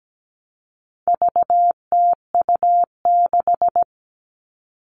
Morse code, heard as a keyed tone, about 17 wpm, 700 Hz. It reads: VTU6
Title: All Cut Numbers - 4-Digit 17wpm